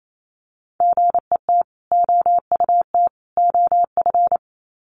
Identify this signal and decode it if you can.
Morse code, heard as a keyed tone, about 28 wpm, 700 Hz. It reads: GET OUT OF
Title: Sets of 3 Words 28wpm